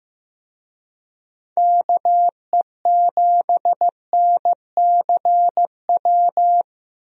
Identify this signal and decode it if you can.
Morse code, heard as a keyed tone, about 15 wpm, 700 Hz. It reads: KE7NCW
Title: USA Call Signs - 6 Characters - 15wpm